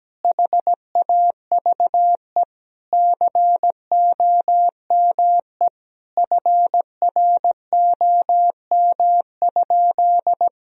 Morse code, heard as a keyed tone, about 17 wpm, 700 Hz. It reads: HAVE COME FROM?
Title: Sentences from Top 1000 Words - Review Part 2 - 17wpm